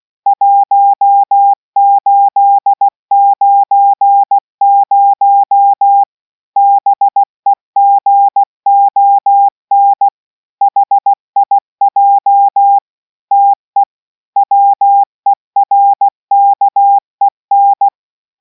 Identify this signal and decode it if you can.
Morse code, heard as a keyed tone, about 16 wpm, 800 Hz. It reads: 1890 BEGON HIJ TE WERKEN